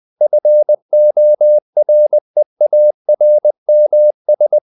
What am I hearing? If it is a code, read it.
Morse code, heard as a keyed tone, about 20 wpm, 600 Hz. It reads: FOREARMS
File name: the old man and the sea part 2 20wpm